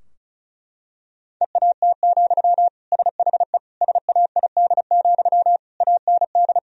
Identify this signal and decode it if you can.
Morse code, heard as a keyed tone, about 34 wpm, 700 Hz. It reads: EAT, SHE SAID, AND
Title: Sentences from Top 1000 Words - Review Part 2 - 35wpm